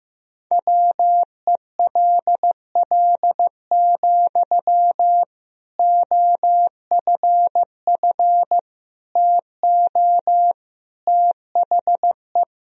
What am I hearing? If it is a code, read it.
Morse code, heard as a keyed tone, about 16 wpm, 700 Hz. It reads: WELL, OFF TO THE
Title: Sentences from Top 1000 Words - Review Part 3 - 15wpm